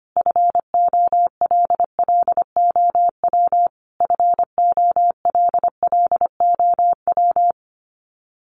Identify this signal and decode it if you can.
Morse code, heard as a keyed tone, about 25 wpm, 700 Hz. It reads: FOLLOW FOLLOW
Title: Top 200 Words Repeated 25wpm